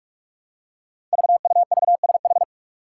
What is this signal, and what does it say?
Morse code, heard as a keyed tone, about 44 wpm, 700 Hz. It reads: VUVSH
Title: Five Challenging Letter-Number Combinations 45wpm